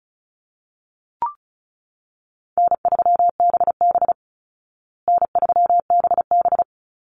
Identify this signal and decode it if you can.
Morse code, heard as a keyed tone, about 34 wpm, 700 Hz. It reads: N3BB N3BB
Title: Call Signs Repeated 3x - 35wpm